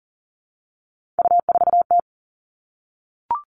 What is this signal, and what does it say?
Morse code, heard as a keyed tone, about 40 wpm, 700 Hz. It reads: U4T
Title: All Cut Numbers - 3-Digit 40wpm